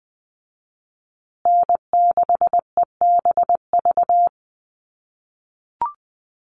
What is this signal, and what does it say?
Morse code, heard as a keyed tone, about 19 wpm, 700 Hz. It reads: N6EBV
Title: USA Call Signs Repeated 3x 20wpm